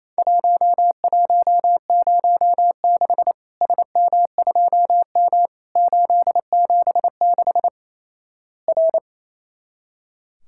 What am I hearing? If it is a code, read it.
Morse code, heard as a keyed tone, about 28 wpm, 700 Hz. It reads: 1106 SM2M 876 R
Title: Worked All Europe DX Contest - QTC Exchange - 28wpm